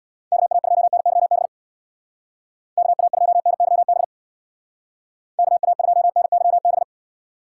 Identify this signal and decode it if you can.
Morse code, heard as a keyed tone, about 60 wpm, 700 Hz. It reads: KA2NYB KA2NYB KA2NYB
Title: Speed Racing - USA 2x3 Call Signs 50wpm